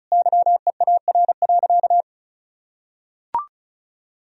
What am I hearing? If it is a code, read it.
Morse code, heard as a keyed tone, about 34 wpm, 700 Hz. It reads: YEAR.
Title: Sentences from Top 1000 Words - Review Part 4 - 35wpm